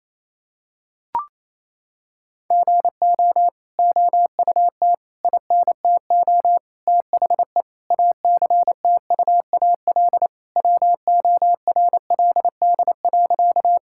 Morse code, heard as a keyed tone, about 28 wpm, 700 Hz. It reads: GO OUT INTO THE ACTUAL WORLD.
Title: Sentences from Top 1000 Words - Review Part 5 - 28wpm